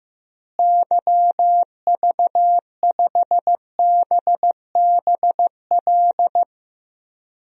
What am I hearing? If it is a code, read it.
Morse code, heard as a keyed tone, about 15 wpm, 700 Hz. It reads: YV5BBL
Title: DXCC-100 Call Signs - 6 Characters - Repeated 3x - 15wpm